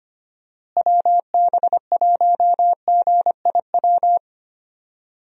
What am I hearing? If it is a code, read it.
Morse code, heard as a keyed tone, about 25 wpm, 700 Hz. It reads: WB1GIW
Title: USA Call Signs - 6 Characters - Repeated 3x - 25wpm